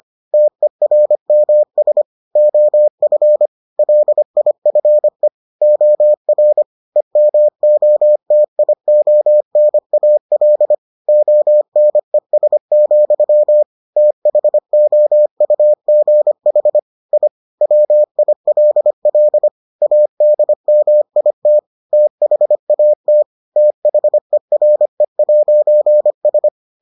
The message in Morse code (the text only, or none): TERMS OF LIFE OR EMOTIONAL ONES, THOUGH I WILL ADMIT THAT THERE'S